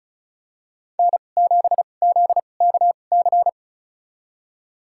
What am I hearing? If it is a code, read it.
Morse code, heard as a keyed tone, about 34 wpm, spaced as for 25 wpm, 700 Hz. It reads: N7ZKC
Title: USA Call Signs - 5 Characters - 2x Character Spacing - 35wpm